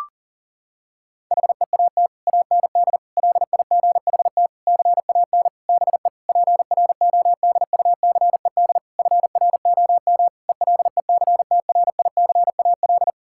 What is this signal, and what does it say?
Morse code, heard as a keyed tone, about 40 wpm, 700 Hz. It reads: HEAT AND LIGHT CAN BE PRODUCED FROM ELECTRICAL
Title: Sentences from Top 1000 Words - Review Part 6 - 40wpm